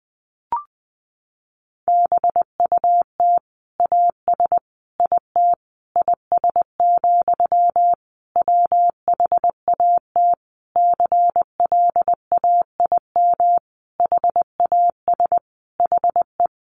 Morse code, heard as a keyed tone, about 20 wpm, 700 Hz. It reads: BUT AS IT IS, WHAT CLAIM HAS HE
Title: Sentences from Top 1500 Words 20wpm